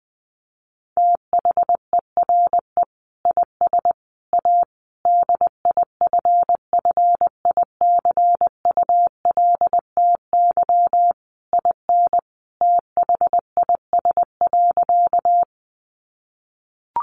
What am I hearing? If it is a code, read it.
Morse code, heard as a keyed tone, about 20 wpm, 700 Hz. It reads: THERE IS A DIFFICULTY IN THIS.
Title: Sentences from Top 1500 Words 20wpm